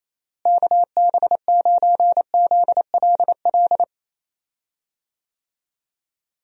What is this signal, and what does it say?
Morse code, heard as a keyed tone, about 28 wpm, 700 Hz. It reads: KB9ZLL
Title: USA Call Signs - 6 Characters - Repeated 3x - 28wpm